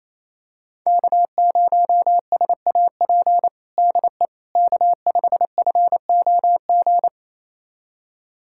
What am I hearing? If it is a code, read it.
Morse code, heard as a keyed tone, about 27 wpm, 700 Hz. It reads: K0SAP DE K5FOG
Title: USA Callsign DE Callsign - 5 Characters - 28wpm